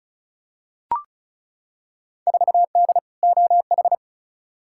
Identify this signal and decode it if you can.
Morse code, heard as a keyed tone, about 36 wpm, 690 Hz. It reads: 4D OH
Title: Field Day Exchanges Encore - 35wpm